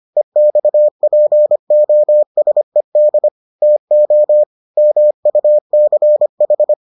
Morse code, heard as a keyed tone, about 25 wpm, 600 Hz. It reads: EXPOSED TO MUCH